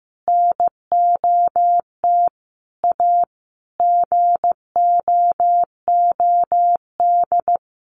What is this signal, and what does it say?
Morse code, heard as a keyed tone, about 15 wpm, 700 Hz. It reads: NOT A GOOD
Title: Sentences from Top 1100 Words 15wpm